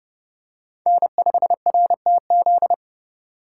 Morse code, heard as a keyed tone, about 32 wpm, 700 Hz. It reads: N5RTZ
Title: USA Call Signs - 4x Character Spacing - 30wpm